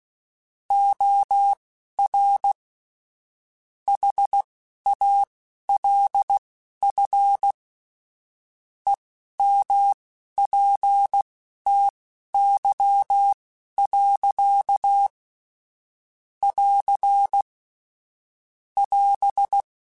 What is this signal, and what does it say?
Morse code, heard as a keyed tone, about 16 wpm, spaced as for 11 wpm, 800 Hz. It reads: OR HALF EMPTY. <AR> <AS>